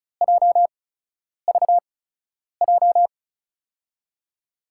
Morse code, heard as a keyed tone, about 36 wpm, spaced as for 10 wpm, 700 Hz. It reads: JVJ
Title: USA Call Signs - 8x Character Spacing - 35wpm